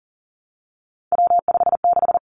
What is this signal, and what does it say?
Morse code, heard as a keyed tone, about 38 wpm, 700 Hz. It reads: W56
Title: Three Challenging Letter-Number Combinations 40wpm